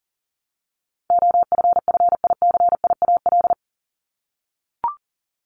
Morse code, heard as a keyed tone, about 37 wpm, 700 Hz. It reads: OFFICIAL
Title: Top 751-1000 Words 40wpm